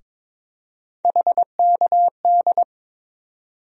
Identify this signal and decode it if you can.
Morse code, heard as a keyed tone, about 22 wpm, 700 Hz. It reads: HKD
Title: USA Call Sign Postfixes - 3 Letters 22wpm